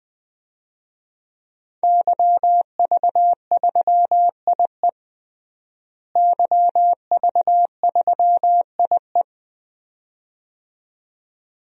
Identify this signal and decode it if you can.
Morse code, heard as a keyed tone, about 20 wpm, 700 Hz. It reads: YV3IE YV3IE
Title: DXCC-100 Call Signs - 5 Characters - Repeated 2x - 20wpm